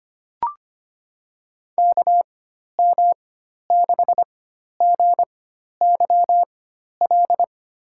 Morse code, heard as a keyed tone, about 25 wpm, spaced as for 12 wpm, 700 Hz. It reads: KM6GYL
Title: USA Call Signs - 6 Characters - 4x Character Spacing - 25wpm